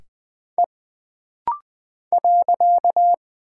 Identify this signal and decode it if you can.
Morse code, heard as a keyed tone, about 20 wpm, 700 Hz. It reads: E .
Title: Single Character (Letter:Number:Punctuation:Prosign) - Rapid-Fire - 20wpm